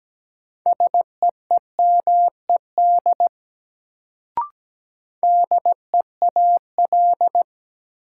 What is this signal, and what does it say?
Morse code, heard as a keyed tone, about 17 wpm, 700 Hz. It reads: SEEMED DEAL
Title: Sets of 3 Words - New Word Drill - 17wpm